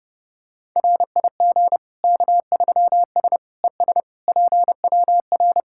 Morse code, heard as a keyed tone, about 30 wpm, 700 Hz. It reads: RIG K3S ES PWR
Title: Practice QSO - Exchange 2 - Info - 30wpm